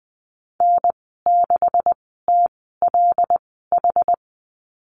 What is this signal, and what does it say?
Morse code, heard as a keyed tone, about 20 wpm, spaced as for 15 wpm, 700 Hz. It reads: N6TLH
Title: 140 Lesson 23g - Letter f - Call Signs 20wpm